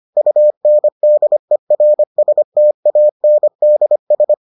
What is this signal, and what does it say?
Morse code, heard as a keyed tone, about 25 wpm, 600 Hz. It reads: UNDERSTANDS